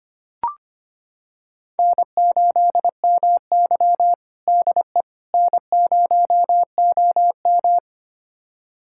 Morse code, heard as a keyed tone, about 25 wpm, 700 Hz. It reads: N8MY DE N0OM
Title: USA Callsign DE Callsign - 4 Characters - 25wpm